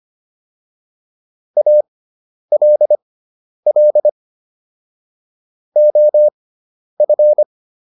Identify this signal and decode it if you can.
Morse code, heard as a keyed tone, about 25 wpm, spaced as for 10 wpm, 600 Hz. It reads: ALL OF